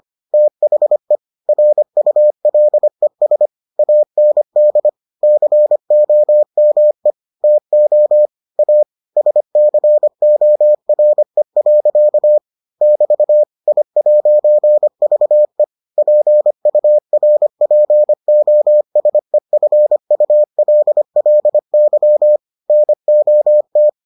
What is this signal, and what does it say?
Morse code, heard as a keyed tone, about 25 wpm, 600 Hz. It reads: THE RULES AND COME TO A SCORE. <BT> I'VE PURPOSEFULLY NOT